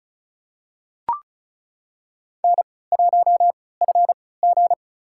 Morse code, heard as a keyed tone, about 34 wpm, spaced as for 20 wpm, 700 Hz. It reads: N1FG
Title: USA Call Signs - 4 Characters - 3x Character Spacing - 35wpm